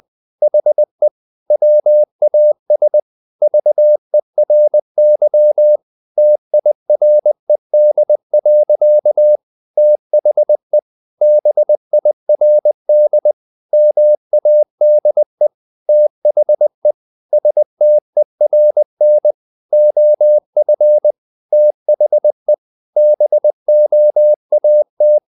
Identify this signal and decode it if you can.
Morse code, heard as a keyed tone, about 20 wpm, 600 Hz. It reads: HE WAS VERY TIRED. THE BIRD MADE THE STERN OF THE BOAT
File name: the old man and the sea / the old man and the sea part 1 20wpm